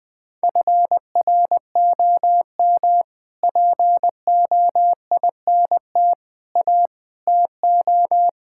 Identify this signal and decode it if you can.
Morse code, heard as a keyed tone, about 20 wpm, 700 Hz. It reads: FROM POINT A TO
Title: Sets of 6 Words - 20wpm